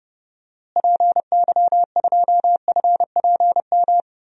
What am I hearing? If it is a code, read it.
Morse code, heard as a keyed tone, about 30 wpm, 700 Hz. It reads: PY2FPM
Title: DXCC-100 Call Signs - 6 Characters - Repeated 2x - 30wpm